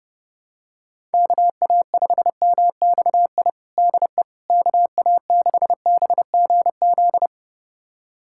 Morse code, heard as a keyed tone, about 30 wpm, 700 Hz. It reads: KA5MXI DE KA6BGZ
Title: USA Callsign DE Callsign - 6 Characters - 30wpm